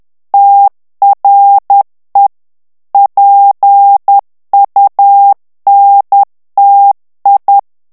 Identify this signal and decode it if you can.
Morse code, heard as a keyed tone, about 10 wpm, 800 Hz. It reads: TRE PUNTI
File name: ITALIEN texte clair 2 vitesse 10 mots minute